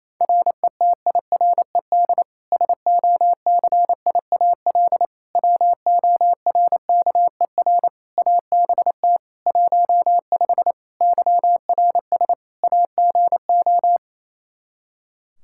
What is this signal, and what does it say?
Morse code, heard as a keyed tone, about 27 wpm, 700 Hz. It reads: RETIRED SOCIAL WORKER ABT 15 YRS AGO
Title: K1USN SST QSO Practice 28wpm